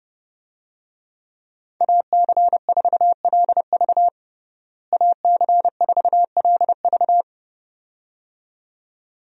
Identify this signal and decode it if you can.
Morse code, heard as a keyed tone, about 31 wpm, 700 Hz. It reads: AC4LV AC4LV
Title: USA Call Signs - 5 Characters - Repeated 2x - 30wpm